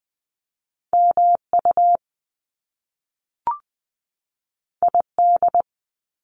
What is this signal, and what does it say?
Morse code, heard as a keyed tone, about 20 wpm, 700 Hz. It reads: MU ID
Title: Top Two Letter Combinations in Words 20wpm